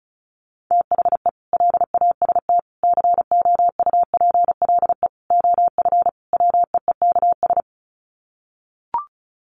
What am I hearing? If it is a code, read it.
Morse code, heard as a keyed tone, about 34 wpm, 700 Hz. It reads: THE LAST COUPLE OF WEEKS
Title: Sets of 5 Words Encore2 35wpm